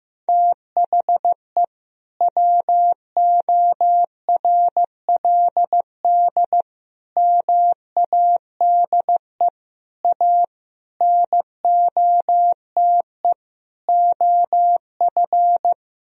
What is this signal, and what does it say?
Morse code, heard as a keyed tone, about 15 wpm, 700 Hz. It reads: THE WORLD MADE A NOTE OF
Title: Sentences from Top 1000 Words - Review Part 2 - 15wpm